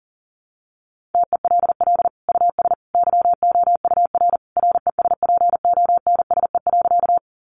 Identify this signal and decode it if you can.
Morse code, heard as a keyed tone, about 40 wpm, 700 Hz. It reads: TELL US YOUR RESPONSE.
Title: Sentences from Top 1400 Words 40wpm